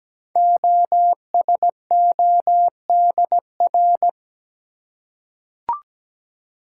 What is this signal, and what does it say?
Morse code, heard as a keyed tone, about 17 wpm, 700 Hz. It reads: OSODR
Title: Code Groups - 5 Letters - 17wpm